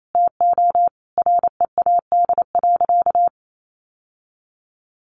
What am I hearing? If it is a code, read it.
Morse code, heard as a keyed tone, about 28 wpm, 700 Hz. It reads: TO READ.
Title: Sentences from Top 1300 Words 28wpm